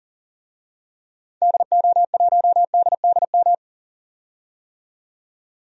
Morse code, heard as a keyed tone, about 40 wpm, 700 Hz. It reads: DO1DDM
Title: DXCC-100 Call Signs - 6 Characters - 40wpm